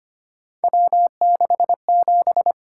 Morse code, heard as a keyed tone, about 29 wpm, 700 Hz. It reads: W67
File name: Speed-Racing - Three Letter-Number Combinations - 25wpm